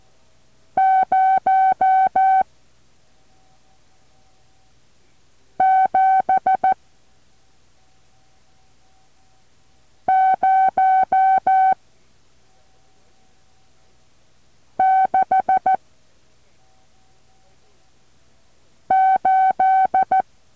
Morse code, heard as a keyed tone, about 14 wpm, 760 Hz. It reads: 0 7 0 6 8
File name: intl morse code2